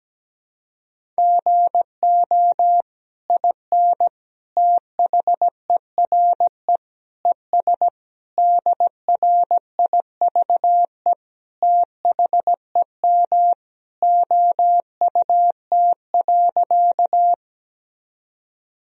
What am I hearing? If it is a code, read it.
Morse code, heard as a keyed tone, about 17 wpm, 700 Hz. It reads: GO IN THERE ES DRIVE THEM OUT.
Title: Sentences from Top 500 Words and Abbreviations 17wpm